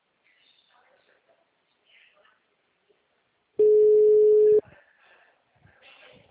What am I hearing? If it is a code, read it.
Ringback tone (Europe, 425 Hz), with speech.
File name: out- -2002- - - .114